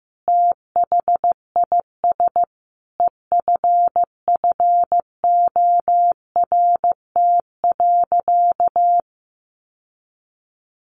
Morse code, heard as a keyed tone, about 15 wpm, 700 Hz. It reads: THIS EFFORT.
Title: Sentences from Top 1400 Words 15wpm